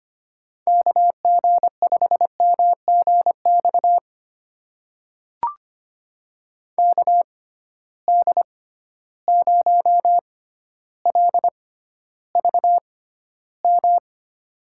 Morse code, heard as a keyed tone, about 25 wpm, spaced as for 13 wpm, 700 Hz. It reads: KG5MGX KD0LVM
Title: USA Call Signs - 6 Characters - 6x Character Spacing - 25wpm